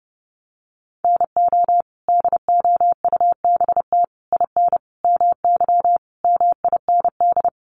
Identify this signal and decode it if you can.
Morse code, heard as a keyed tone, about 30 wpm, 700 Hz. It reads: NO DOUBT IN MY MIND
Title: Sets of 5 Words Encore2 30wpm